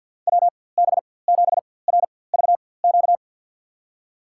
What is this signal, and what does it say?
Morse code, heard as a keyed tone, about 52 wpm, spaced as for 23 wpm, 700 Hz. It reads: WB7RVQ
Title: USA Call Signs - 6 Characters - 4x Character Spacing - 50wpm